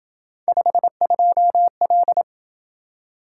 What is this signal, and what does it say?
Morse code, heard as a keyed tone, about 27 wpm, 700 Hz. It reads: 52L